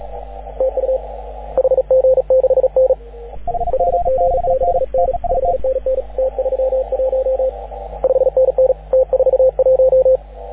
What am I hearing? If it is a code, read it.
Morse code, heard as a keyed tone, about 36 wpm, 530 Hz, with radio noise. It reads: TU HG6N HG6N 5NN T31 5NN T41
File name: AUDIO2019 / HG6N CW SSB MIX